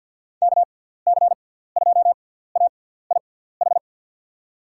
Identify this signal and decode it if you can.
Morse code, heard as a keyed tone, about 50 wpm, spaced as for 17 wpm, 700 Hz. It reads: KC2AIH
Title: USA Call Signs - 6 Characters - 6x Character Spacing - 50wpm